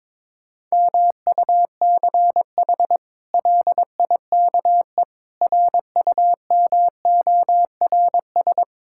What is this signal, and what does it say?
Morse code, heard as a keyed tone, about 22 wpm, 700 Hz. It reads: MUCH LIKE RUMORS
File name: practice / 22wpm / Sets of 8 Words - 22wpm